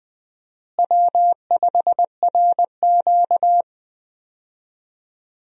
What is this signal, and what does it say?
Morse code, heard as a keyed tone, about 20 wpm, 700 Hz. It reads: W5RQ
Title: USA Call Signs - 4 Characters - Repeated 2x - 20wpm